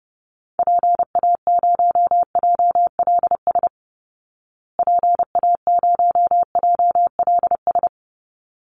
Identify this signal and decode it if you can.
Morse code, heard as a keyed tone, about 28 wpm, 700 Hz. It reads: PA0JLS PA0JLS
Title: Call Signs Repeated 3x - 30wpm